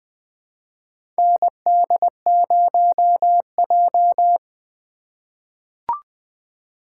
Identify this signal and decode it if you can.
Morse code, heard as a keyed tone, about 20 wpm, 700 Hz. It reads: ND0J
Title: USA Call Signs - 4x Character Spacing - 20wpm